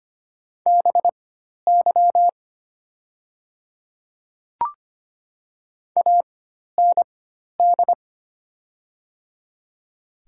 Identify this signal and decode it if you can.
Morse code, heard as a keyed tone, about 25 wpm, spaced as for 12 wpm, 700 Hz. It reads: BY AND
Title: 4x Character Spacing - Top 100 Words - 25wpm